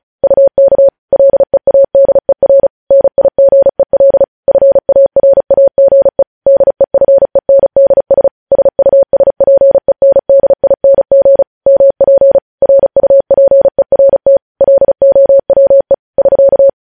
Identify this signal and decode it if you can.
Morse code, heard as a keyed tone, about 36 wpm, 550 Hz. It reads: UK LEADER NIGEL FARAGE DEFENDS SUSPENDING MP RUPERT LOWE <SK>
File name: podcast / News headlines-1x-35x35